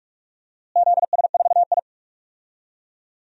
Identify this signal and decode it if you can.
Morse code, heard as a keyed tone, about 46 wpm, 700 Hz. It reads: ZS4I
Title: DXCC-100 Call Signs - 4 Characters - Repeated 2x - 45wpm